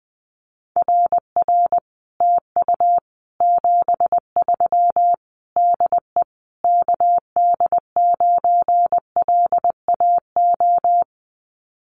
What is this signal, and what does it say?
Morse code, heard as a keyed tone, about 19 wpm, 700 Hz. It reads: RR TU 73 DE KD9LAO
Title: Common QSO Phrases Encore 20wpm